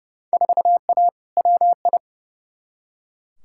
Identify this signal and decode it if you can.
Morse code, heard as a keyed tone, about 32 wpm, 700 Hz. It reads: 4A WI
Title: Field Day Exchanges - 30wpm